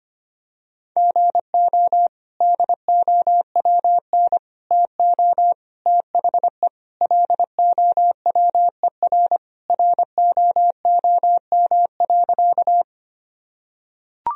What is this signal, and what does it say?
Morse code, heard as a keyed tone, about 25 wpm, 700 Hz. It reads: GO DOWN TO THE LOWER ROOM.
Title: Sentences from Top 1000 Words - Review Part 4 - 25wpm